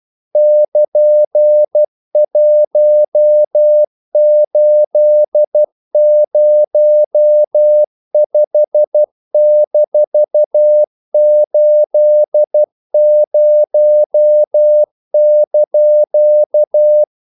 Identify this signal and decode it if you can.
Morse code, heard as a keyed tone, about 12 wpm, 600 Hz. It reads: <KN>1805-80)
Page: English quotations in CW (telegraphy - morse)